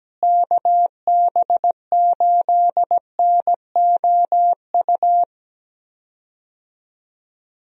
Morse code, heard as a keyed tone, about 17 wpm, 700 Hz. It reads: KB8NOU
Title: USA Call Signs - 6 Characters - Repeated 2x - 17wpm